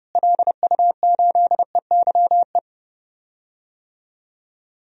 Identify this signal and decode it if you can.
Morse code, heard as a keyed tone, about 32 wpm, 700 Hz. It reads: LU8EYE
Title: DXCC-100 Call Signs - 6 Characters - Repeated 2x - 30wpm